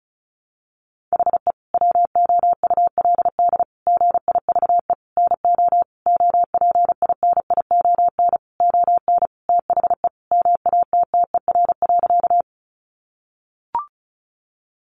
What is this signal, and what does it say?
Morse code, heard as a keyed tone, about 34 wpm, 700 Hz. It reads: HE WOULD GIVE NO OPINION ON THE MATTER.
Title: Sentences from Top 1200 Words 35wpm